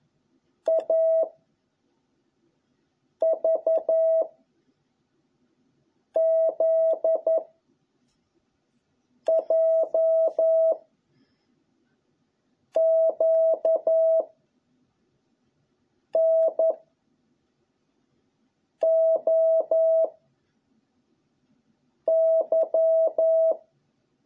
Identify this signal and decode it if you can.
Morse code, heard as a keyed tone, about 11 wpm, 660 Hz. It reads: A V Z J Q N O Y